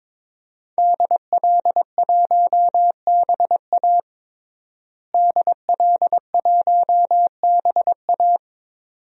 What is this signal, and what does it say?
Morse code, heard as a keyed tone, about 21 wpm, 700 Hz. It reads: DL1BA DL1BA
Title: DXCC-100 Call Signs - 5 Characters - Repeated 2x - 22wpm